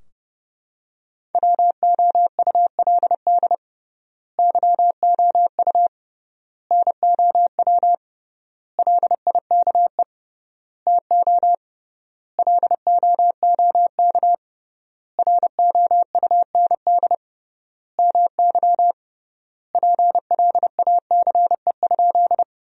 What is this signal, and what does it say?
Morse code, heard as a keyed tone, about 32 wpm, 700 Hz. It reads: WOULD YOU NOW LIKE TO LOOK ROUND MY PLACE?
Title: 3x Word-Spacing - Sentences from Top 200 Words - 30wpm